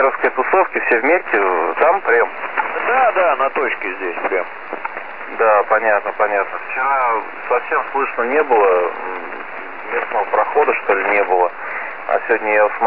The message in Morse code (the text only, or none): none